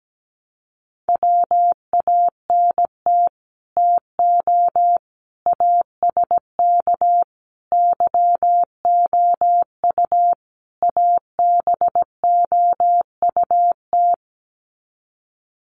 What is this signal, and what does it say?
Morse code, heard as a keyed tone, about 17 wpm, 700 Hz. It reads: WANT TO ASK YOU ABOUT
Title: Sets of 5 Words Encore2 17wpm